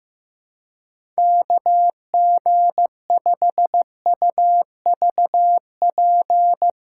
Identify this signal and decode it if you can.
Morse code, heard as a keyed tone, about 15 wpm, 700 Hz. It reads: KG5UVP
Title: USA Callsign DE Callsign 15wpm